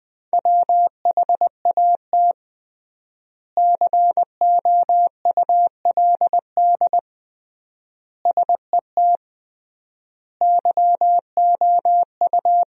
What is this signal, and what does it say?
Morse code, heard as a keyed tone, about 20 wpm, 700 Hz. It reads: WHAT COULD SET YOU
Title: 3x Word-Spacing - Sentences from Top 200 Words - 20wpm